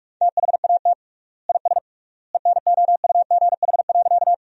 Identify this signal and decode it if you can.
Morse code, heard as a keyed tone, about 46 wpm, 700 Hz. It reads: THAT IS ENOUGH.
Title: 3x Word-Spacing - Sentences from Top 400 Words - 45wpm